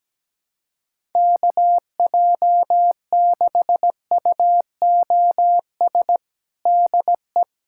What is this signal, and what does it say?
Morse code, heard as a keyed tone, about 17 wpm, 700 Hz. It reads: KJ6UOS DE
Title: USA Callsign DE Callsign - 6 Characters - 17wpm